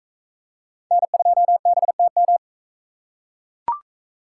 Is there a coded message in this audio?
Yes: Morse code from 0.9 to 2.4 s, heard as a keyed tone, about 40 wpm, 700 Hz. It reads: N2BTM